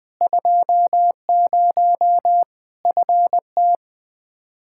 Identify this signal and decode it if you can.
Morse code, heard as a keyed tone, about 21 wpm, 700 Hz. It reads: 20 FT
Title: Practice QSO - Exchange 2 - Info - 20wpm